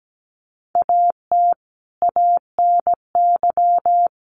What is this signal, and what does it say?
Morse code, heard as a keyed tone, about 17 wpm, 700 Hz. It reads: AT ANY
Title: Sentences from Top 900 Words 17wpm